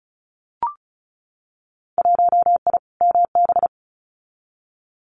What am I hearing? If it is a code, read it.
Morse code, heard as a keyed tone, about 36 wpm, 700 Hz. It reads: 1I MB
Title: Winter Field Day Exchanges Encore 35wpm